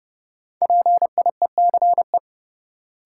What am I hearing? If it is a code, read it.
Morse code, heard as a keyed tone, about 30 wpm, 700 Hz. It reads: PIECE
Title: Sentences from Top 400 Words - New Word Drill - 30wpm